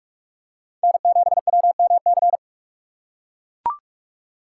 Morse code, heard as a keyed tone, about 41 wpm, 700 Hz. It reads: N7WMC